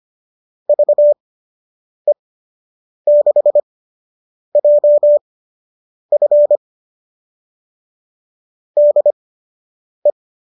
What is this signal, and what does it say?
Morse code, heard as a keyed tone, about 25 wpm, spaced as for 8 wpm, 570 Hz. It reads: VE6JF DE